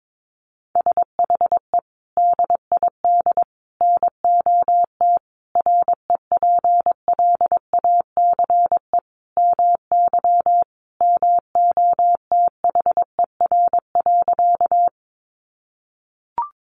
Morse code, heard as a keyed tone, about 22 wpm, 700 Hz. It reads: SHE DID NOT REPLACE MY MOTHER.
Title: Sentences from Top 1400 Words 22wpm